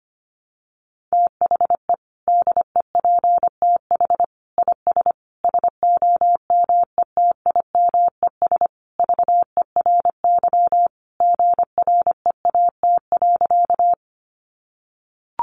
Morse code, heard as a keyed tone, about 25 wpm, 700 Hz. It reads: THE DEPTH IS SOMETIMES VERY GREAT.
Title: Sentences from Top 1200 Words 25wpm